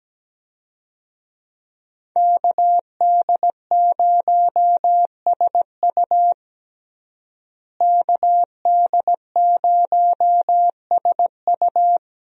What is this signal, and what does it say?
Morse code, heard as a keyed tone, about 17 wpm, 700 Hz. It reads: KD0SU KD0SU
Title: USA Call Signs - 5 Characters - Repeated 2x - 17wpm